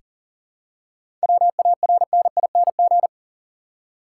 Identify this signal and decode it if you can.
Morse code, heard as a keyed tone, about 40 wpm, 700 Hz. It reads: WARNING
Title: Seven Letter Words - 40wpm